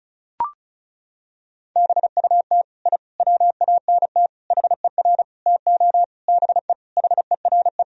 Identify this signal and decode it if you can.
Morse code, heard as a keyed tone, about 36 wpm, 700 Hz. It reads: BUT I WANT HER TO BE HERE
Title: Sentences from Top 1000 Words - Review Part 4 - 35wpm